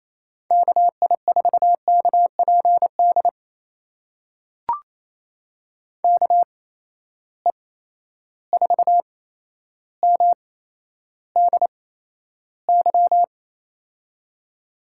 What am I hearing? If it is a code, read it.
Morse code, heard as a keyed tone, about 28 wpm, spaced as for 12 wpm, 700 Hz. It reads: KI4KPD KE4MDY
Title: USA Call Signs - 8x Character Spacing - 28wpm